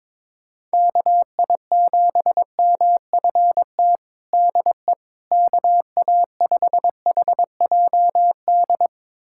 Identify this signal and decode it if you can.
Morse code, heard as a keyed tone, about 22 wpm, 700 Hz. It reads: KI7MFT DE KA5HJD
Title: USA Callsign DE Callsign - 6 Characters - 22wpm